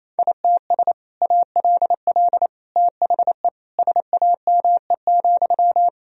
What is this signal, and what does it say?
Morse code, heard as a keyed tone, about 28 wpm, 700 Hz. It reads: ITS ALL THE SAME,
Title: Sentences from Top 1000 Words - Review Part 3 - 28wpm